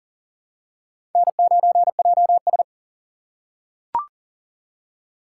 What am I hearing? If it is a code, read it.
Morse code, heard as a keyed tone, about 41 wpm, 690 Hz. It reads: N9JS
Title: USA Call Signs - 4 Characters - 2x Character Spacing - 40wpm